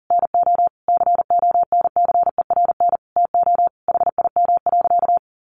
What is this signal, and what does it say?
Morse code, heard as a keyed tone, about 40 wpm, 700 Hz. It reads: NO CONCERN TO HIM.
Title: Sentences from Top 1200 Words 40wpm